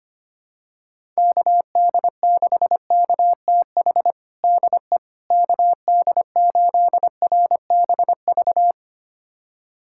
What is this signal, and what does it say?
Morse code, heard as a keyed tone, about 25 wpm, 700 Hz. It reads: KD6KTH DE KD8RBV
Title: USA Callsign DE Callsign - 6 Characters - 25wpm